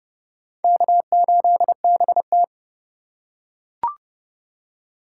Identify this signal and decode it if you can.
Morse code, heard as a keyed tone, about 29 wpm, 700 Hz. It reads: K8BT